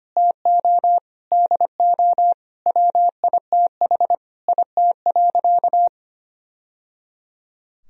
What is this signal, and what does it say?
Morse code, heard as a keyed tone, about 25 wpm, 700 Hz. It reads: TO DO WITH IT.
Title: Sentences from Top 1000 Words - Review Part 2 - 25wpm